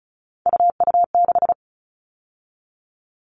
Morse code, heard as a keyed tone, about 34 wpm, 700 Hz. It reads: UU6
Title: All Cut Numbers - 3-Digit 35wpm